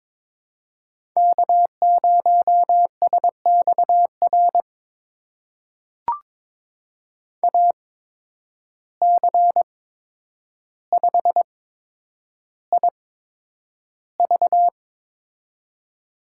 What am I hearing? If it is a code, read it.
Morse code, heard as a keyed tone, about 21 wpm, spaced as for 9 wpm, 700 Hz. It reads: K0SXR AC5IV
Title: USA Call Signs - 5 Characters - 8x Character Spacing - 22wpm